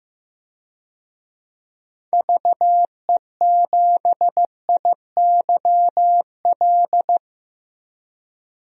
Morse code, heard as a keyed tone, about 15 wpm, 700 Hz. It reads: VE7IYL
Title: DXCC-100 Call Signs - 6 Characters - Repeated 3x - 15wpm